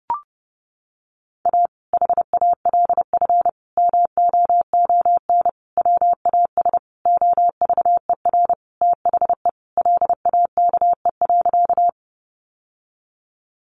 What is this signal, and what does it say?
Morse code, heard as a keyed tone, about 30 wpm, 700 Hz. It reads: A HALF MOON WAS OVER THE LAKE.
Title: Sentences from Top 1100 Words 30wpm